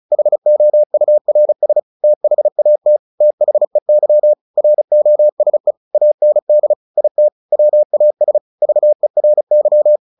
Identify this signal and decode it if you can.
Morse code, heard as a keyed tone, about 35 wpm, 600 Hz. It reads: HOURS THAT THEY ROSE AND IT WAS VERY
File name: the old man and the sea / the old man and the sea 1 35wpm